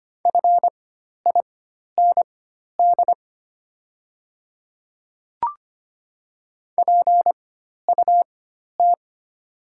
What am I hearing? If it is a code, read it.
Morse code, heard as a keyed tone, about 25 wpm, spaced as for 12 wpm, 700 Hz. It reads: FIND PUT
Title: 4x Character Spacing - Top 500 Words - 25wpm